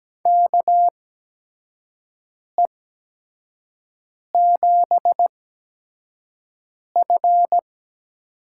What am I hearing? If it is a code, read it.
Morse code, heard as a keyed tone, about 17 wpm, spaced as for 5 wpm, 700 Hz. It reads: KE7F
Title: USA Call Signs - 6 Characters - 8x Character Spacing - 17wpm